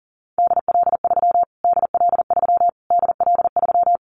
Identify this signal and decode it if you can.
Morse code, heard as a keyed tone, about 40 wpm, 700 Hz. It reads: DL3 DL3 DL3
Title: Callsign Prefixes for Top 25 DXCC - Repeated 3x - 40wpm